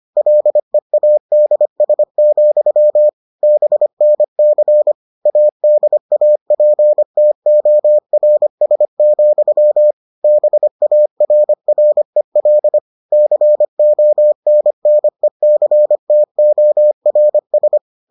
Morse code, heard as a keyed tone, about 25 wpm, 600 Hz. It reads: LEADS, BNC ADAPTORS, BARREL CONNECTORS